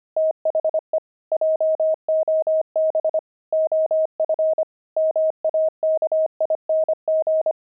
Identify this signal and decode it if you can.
Morse code, heard as a keyed tone, about 25 wpm, 630 Hz. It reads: THE JOB OF MAKING
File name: pangram-fs-25-cs-25-toys